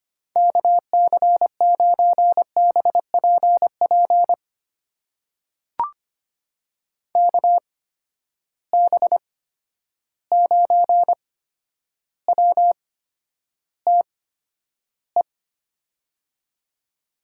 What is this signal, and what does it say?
Morse code, heard as a keyed tone, about 25 wpm, spaced as for 11 wpm, 700 Hz. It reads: KC9BPP KB9WTE
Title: USA Call Signs - 6 Characters - 8x Character Spacing - 25wpm